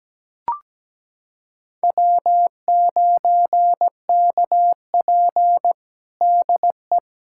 Morse code, heard as a keyed tone, about 17 wpm, 700 Hz. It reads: W9KP DE
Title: USA Callsign DE Callsign - 4 Characters - 17wpm